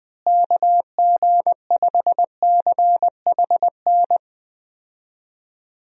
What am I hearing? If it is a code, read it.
Morse code, heard as a keyed tone, about 20 wpm, 700 Hz. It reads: KG5CHN
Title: USA Call Signs - 6 Characters - Repeated 3x - 20wpm